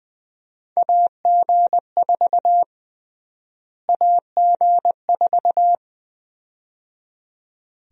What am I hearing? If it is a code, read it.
Morse code, heard as a keyed tone, about 20 wpm, 700 Hz. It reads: AG4 AG4
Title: USA Call Sign Prefixes - Repeated 2x - 20wpm